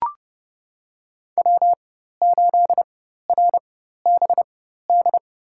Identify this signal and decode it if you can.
Morse code, heard as a keyed tone, about 30 wpm, spaced as for 14 wpm, 700 Hz. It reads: W8RBD
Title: USA Call Signs - 5 Characters - 4x Character Spacing - 30wpm